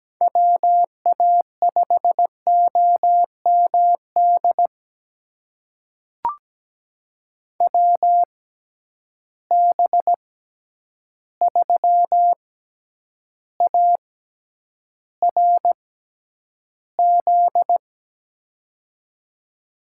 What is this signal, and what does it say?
Morse code, heard as a keyed tone, about 17 wpm, spaced as for 9 wpm, 700 Hz. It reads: WA5OMD WB3ARZ
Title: USA Call Signs - 6 Characters - 6x Character Spacing - 17wpm